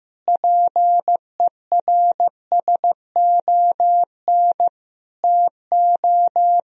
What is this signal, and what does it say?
Morse code, heard as a keyed tone, about 16 wpm, 700 Hz. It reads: PERSON TO
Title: Sentences from Top 1000 Words - Review Part 3 - 15wpm